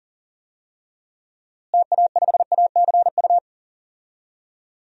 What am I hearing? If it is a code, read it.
Morse code, heard as a keyed tone, about 40 wpm, 700 Hz. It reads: TA5ACU
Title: DXCC-100 Call Signs - 6 Characters - Repeated 2x - 40wpm